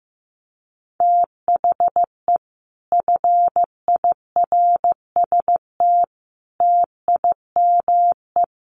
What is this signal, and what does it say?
Morse code, heard as a keyed tone, about 15 wpm, 700 Hz. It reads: THE FIRST TIME
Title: Sets of 5 Words Encore 15wpm